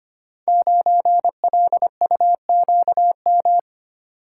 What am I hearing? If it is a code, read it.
Morse code, heard as a keyed tone, about 25 wpm, 700 Hz. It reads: 9LUQM
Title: Code Groups - 5 Letter:Numbers - 25wpm